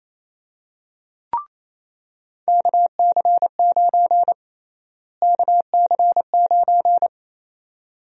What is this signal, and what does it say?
Morse code, heard as a keyed tone, about 28 wpm, 700 Hz. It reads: KC9 KC9
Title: USA Call Sign Prefixes - Repeated 2x - 28wpm